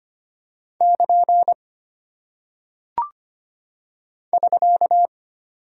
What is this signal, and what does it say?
Morse code, heard as a keyed tone, about 25 wpm, 700 Hz. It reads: <KN> <SK>
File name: Single Punctuation-Prosign - 25wpm